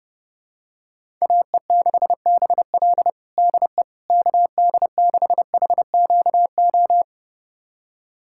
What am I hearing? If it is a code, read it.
Morse code, heard as a keyed tone, about 30 wpm, 700 Hz. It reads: AE6BL DE KD6HQO
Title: USA Callsign DE Callsign 30wpm